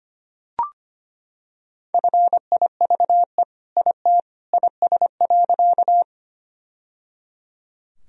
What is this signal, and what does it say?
Morse code, heard as a keyed tone, about 25 wpm, 700 Hz. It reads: FIVE IT IS.
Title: Sentences from Top 1000 Words - Review Part 2 - 25wpm